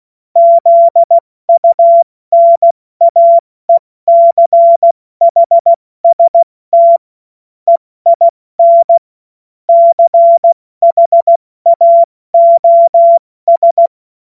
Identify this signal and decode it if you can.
Morse code, heard as a keyed tone, about 16 wpm, 680 Hz. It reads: ZUNAECHST EIN CHAOS
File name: Morsekurs / Witze / Witz 21.November 2023 70 F80